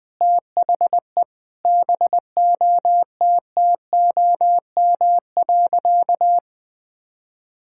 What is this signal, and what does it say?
Morse code, heard as a keyed tone, about 20 wpm, 700 Hz. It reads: THE BOTTOM.
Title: Sentences from Top 1000 Words - Review Part 6 - 20wpm